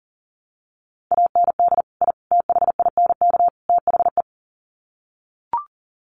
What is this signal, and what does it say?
Morse code, heard as a keyed tone, about 40 wpm, 700 Hz. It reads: AND I THINK THE
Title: Sets of 4 Words 40wpm